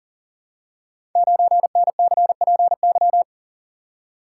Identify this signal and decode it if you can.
Morse code, heard as a keyed tone, about 42 wpm, 690 Hz. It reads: 9NCPY
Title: Code Groups - 5 Letter:Numbers - 40wpm